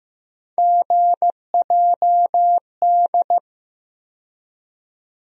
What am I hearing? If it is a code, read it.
Morse code, heard as a keyed tone, about 15 wpm, 700 Hz. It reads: GJD
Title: USA Call Sign Postfixes - 3 Letters 15wpm